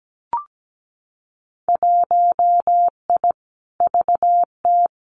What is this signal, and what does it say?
Morse code, heard as a keyed tone, about 17 wpm, 700 Hz. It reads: 1I VT
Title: Winter Field Day Exchanges Encore 17wpm